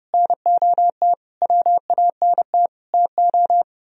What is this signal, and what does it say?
Morse code, heard as a keyed tone, about 39 wpm, 700 Hz. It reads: NOT WANT TO